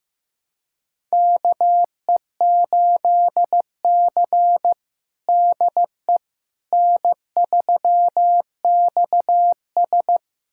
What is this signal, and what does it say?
Morse code, heard as a keyed tone, about 15 wpm, 700 Hz. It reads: KE8C DE N3XS
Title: USA Callsign DE Callsign - 4 Characters - 15wpm